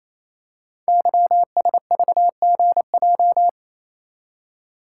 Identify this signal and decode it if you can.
Morse code, heard as a keyed tone, about 29 wpm, 700 Hz. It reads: YSVGJ
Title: Code Groups - 5 Letters - 28wpm